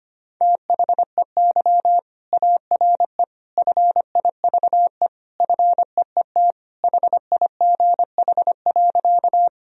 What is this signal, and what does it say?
Morse code, heard as a keyed tone, about 25 wpm, 700 Hz. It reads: THEY ARE FIVE FEET HIGH.
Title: Sentences from Top 1000 Words - Review Part 2 - 25wpm